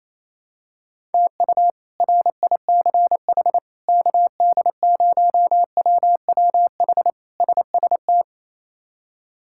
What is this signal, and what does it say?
Morse code, heard as a keyed tone, about 28 wpm, 700 Hz. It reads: TU RICH KD0WWH SST
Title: K1USN SST QSO Practice - 28wpm